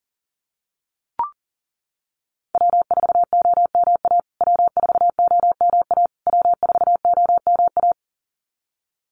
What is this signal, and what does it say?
Morse code, heard as a keyed tone, about 40 wpm, 700 Hz. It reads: W4OMA W4OMA W4OMA
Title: USA Call Signs Repeated 3x 40wpm